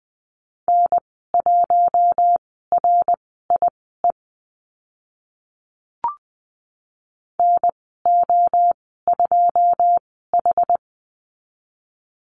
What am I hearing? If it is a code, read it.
Morse code, heard as a keyed tone, about 20 wpm, spaced as for 14 wpm, 700 Hz. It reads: N1RIE NO2H
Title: 082 Lesson 14g - Number 2 - Call Signs 20wpm